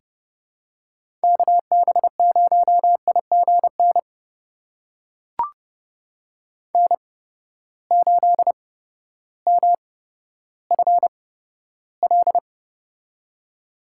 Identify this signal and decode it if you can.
Morse code, heard as a keyed tone, about 31 wpm, spaced as for 14 wpm, 700 Hz. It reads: KB0IGN N8MFL
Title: USA Call Signs - 8x Character Spacing - 30wpm